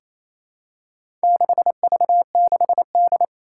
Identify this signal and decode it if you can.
Morse code, heard as a keyed tone, about 28 wpm, 700 Hz. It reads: 6V6D
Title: Four Challenging Letter-Number Combinations 28wpm